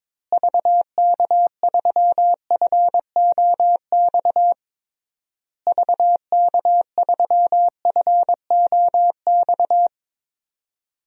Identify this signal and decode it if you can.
Morse code, heard as a keyed tone, about 22 wpm, 700 Hz. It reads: VK3FOX VK3FOX
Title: DXCC-100 Call Signs - 6 Characters - Repeated 3x - 22wpm